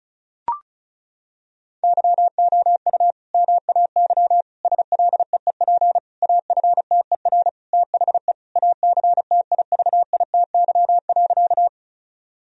Morse code, heard as a keyed tone, about 36 wpm, 700 Hz. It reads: YOU MAY SLEEP AFTER THE ACTIVITY.
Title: Sentences from Top 1000 Words - Review Part 4 - 35wpm